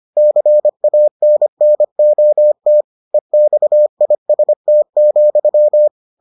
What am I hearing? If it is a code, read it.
Morse code, heard as a keyed tone, about 25 wpm, 600 Hz. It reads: CANNOT EXIST,